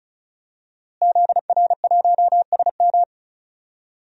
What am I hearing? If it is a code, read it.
Morse code, heard as a keyed tone, about 34 wpm, 690 Hz. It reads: ZR1SM
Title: DXCC-100 Call Signs - 5 Characters - Repeated 3x - 35wpm